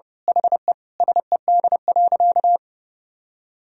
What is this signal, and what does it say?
Morse code, heard as a keyed tone, about 32 wpm, 700 Hz. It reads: HE SED.
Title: Sentences from Top 100 Words and Abbreviations 30wpm